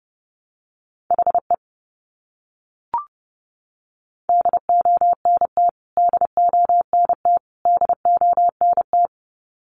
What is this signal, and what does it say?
Morse code, heard as a keyed tone, about 29 wpm, 700 Hz. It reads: HE DONT DONT DONT
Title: Top 500 Words Repeated 30wpm